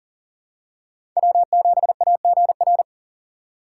Morse code, heard as a keyed tone, about 40 wpm, 700 Hz. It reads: W7AGR
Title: USA Call Signs - 3x Character Spacing - 40wpm